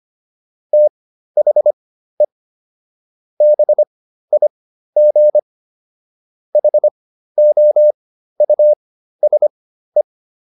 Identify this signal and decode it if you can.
Morse code, heard as a keyed tone, about 25 wpm, spaced as for 13 wpm, 600 Hz. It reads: THE BIG HOUSE
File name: PR404 13